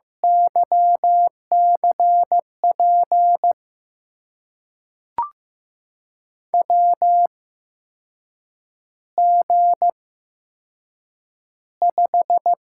Morse code, heard as a keyed tone, about 15 wpm, spaced as for 6 wpm, 700 Hz. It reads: YCP WG5
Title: USA Call Signs - 8x Character Spacing - 15wpm